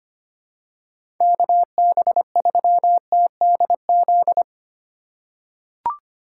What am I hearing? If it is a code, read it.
Morse code, heard as a keyed tone, about 25 wpm, 700 Hz. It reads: KB3TDZ
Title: USA Call Signs - 6 Characters - 3x Character Spacing - 25wpm